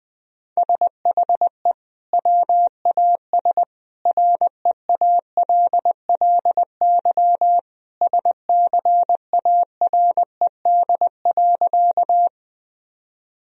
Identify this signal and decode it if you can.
Morse code, heard as a keyed tone, about 21 wpm, 700 Hz. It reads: SHE WAS REALLY SCARED.
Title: Sentences from Top 1000 Words - Review Part 4 - 20wpm